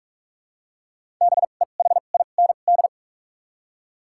Morse code, heard as a keyed tone, about 46 wpm, spaced as for 32 wpm, 700 Hz. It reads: BEHIND
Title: 2x Character Spacing - Top 500 Words - 45wpm